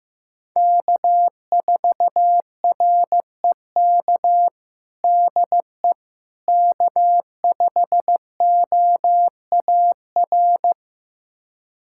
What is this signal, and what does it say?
Morse code, heard as a keyed tone, about 16 wpm, 700 Hz. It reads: K4REK DE K5OAR
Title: USA Callsign DE Callsign - 5 Characters - 15wpm